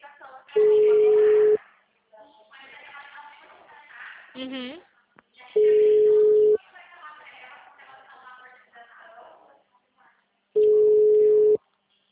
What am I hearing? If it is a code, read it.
Ringback tone (Europe, 425 Hz), with speech.